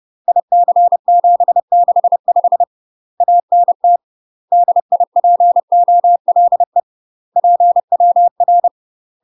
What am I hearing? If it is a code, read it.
Morse code, heard as a keyed tone, about 30 wpm, 700 Hz. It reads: IC765 ANT DIPOLE PWR
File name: UK MASTER QSOs TRAINING / 30WPM / UK MASTER QSOs 0012